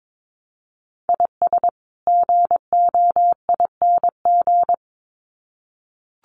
Morse code, heard as a keyed tone, about 22 wpm, 700 Hz. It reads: IS GOING
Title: Sets of 2 Words 22wpm